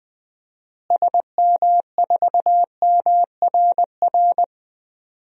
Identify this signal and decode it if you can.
Morse code, heard as a keyed tone, about 20 wpm, 700 Hz. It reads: SM4MRR
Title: DXCC-100 Call Signs - 6 Characters - Repeated 2x - 20wpm